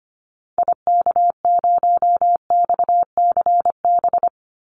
Morse code, heard as a keyed tone, about 25 wpm, 700 Hz. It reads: IK0XCB
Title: Call Signs - 25wpm